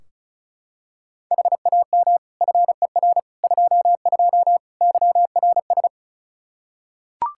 Morse code, heard as a keyed tone, about 36 wpm, 700 Hz. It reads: HAM FER 22 YRS
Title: Practice QSO - Exchange 3 - More Info - 35wpm